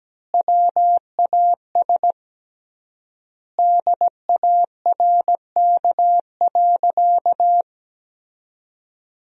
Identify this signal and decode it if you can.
Morse code, heard as a keyed tone, about 17 wpm, 700 Hz. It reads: WAS DARK.
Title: 3x Word-Spacing - Sentences from Top 500 Words - 17wpm